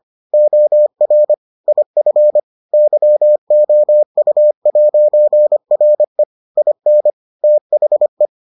Morse code, heard as a keyed tone, about 25 wpm, 600 Hz. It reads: OR IF YOU'RE IN THE